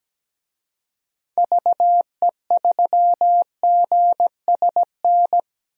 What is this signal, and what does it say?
Morse code, heard as a keyed tone, about 17 wpm, 700 Hz. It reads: VE3GSN
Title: DXCC-100 Call Signs - 6 Characters - Repeated 2x - 17wpm